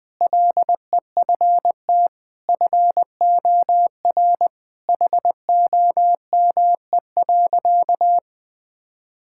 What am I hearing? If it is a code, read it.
Morse code, heard as a keyed tone, about 20 wpm, 700 Hz. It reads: LEFT FOR HOME.
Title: Sentences with 4-Word Phrases - Spoken Beginning - 20wpm